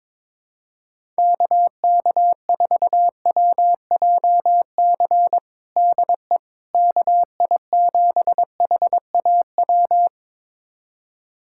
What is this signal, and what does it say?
Morse code, heard as a keyed tone, about 22 wpm, 700 Hz. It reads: KK4WJC DE KI7HAW
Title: USA Callsign DE Callsign - 6 Characters - 22wpm